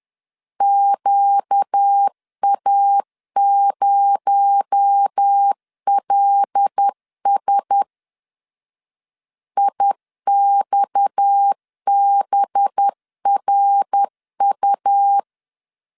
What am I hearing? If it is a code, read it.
Morse code, heard as a keyed tone, about 10 wpm, 800 Hz. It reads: QA0LS IXBRU
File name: LEZIONI / 6 PKX0 / 6 PKX0 10wpm 2707